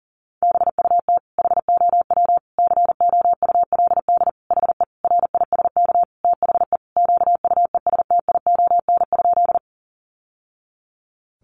Morse code, heard as a keyed tone, about 37 wpm, 700 Hz. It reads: BUT HOW COULD HE RISK THE QUESTION?
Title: Sentences from Top 600 Words 40wpm